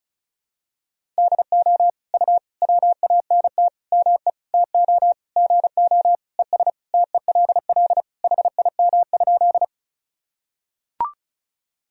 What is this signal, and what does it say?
Morse code, heard as a keyed tone, about 34 wpm, 700 Hz. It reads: DO U WANT ME TO GO ES TELL HIM?
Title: Sentences from Top 200 Words and Abbreviations 35wpm